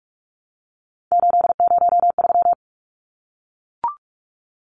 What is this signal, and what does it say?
Morse code, heard as a keyed tone, about 44 wpm, 700 Hz. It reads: 803
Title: Numbers - 3-Digit 45wpm